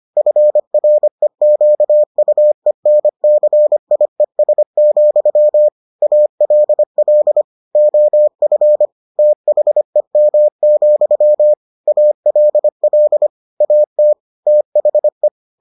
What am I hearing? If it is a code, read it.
Morse code, heard as a keyed tone, about 25 wpm, 600 Hz. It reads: FREQUENCIES, ALL OF THEM, ALL AT THE